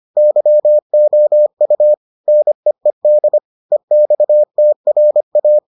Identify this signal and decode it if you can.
Morse code, heard as a keyed tone, about 25 wpm, 600 Hz. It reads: YOU NEED EXTRA